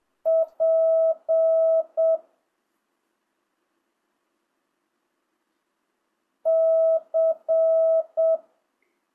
Morse code, heard as a keyed tone, about 7 wpm, 660 Hz. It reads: P C
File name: jh8cbh / 2025cw